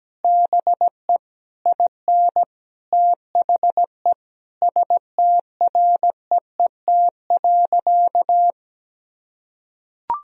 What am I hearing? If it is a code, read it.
Morse code, heard as a keyed tone, about 17 wpm, 700 Hz. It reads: BE IN THE STREET.
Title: Sentences from Top 1000 Words - Review Part 3 - 17wpm